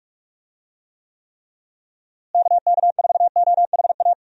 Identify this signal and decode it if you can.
Morse code, heard as a keyed tone, about 46 wpm, 700 Hz. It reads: KK4OHA
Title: USA Call Signs - 6 Characters - Repeated 3x - 45wpm